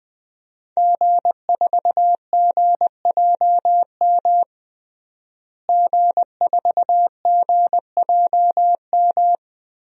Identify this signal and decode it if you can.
Morse code, heard as a keyed tone, about 21 wpm, 700 Hz. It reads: G4GJM G4GJM
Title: DXCC-100 Call Signs - 5 Characters - Repeated 2x - 20wpm